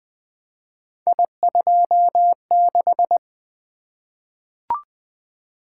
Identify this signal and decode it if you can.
Morse code, heard as a keyed tone, about 20 wpm, 700 Hz. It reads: I26
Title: Three Letter-Number Combinations - 20wpm